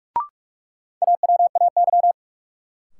Morse code, heard as a keyed tone, about 46 wpm, 700 Hz. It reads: AWAY
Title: Sentences from Top 600 Words - New Word Drill - 45wpm